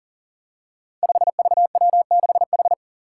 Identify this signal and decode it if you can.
Morse code, heard as a keyed tone, about 40 wpm, 700 Hz. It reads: 5VW6H
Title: Five Challenging Letter-Number Combinations 40wpm